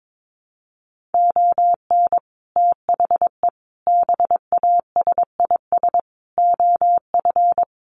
Morse code, heard as a keyed tone, about 22 wpm, 700 Hz. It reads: ON THE BASIS OF
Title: Sets of 5 Words Encore 22wpm